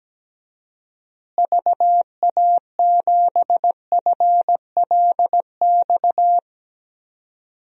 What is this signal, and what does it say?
Morse code, heard as a keyed tone, about 17 wpm, 700 Hz. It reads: VA7FLX
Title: DXCC-100 Call Signs - 6 Characters - Repeated 2x - 17wpm